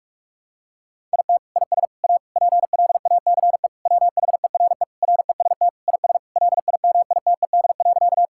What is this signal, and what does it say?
Morse code, heard as a keyed tone, about 46 wpm, 690 Hz. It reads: IT IS A PLACE WHERE REST IS LIMITED.
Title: Sentences from Top 1000 Words - Review Part 6 - 45wpm